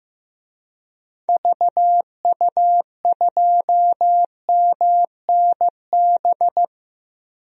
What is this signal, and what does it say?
Morse code, heard as a keyed tone, about 15 wpm, 700 Hz. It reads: VU2MNB
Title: DXCC-100 Call Signs - 6 Characters - 15wpm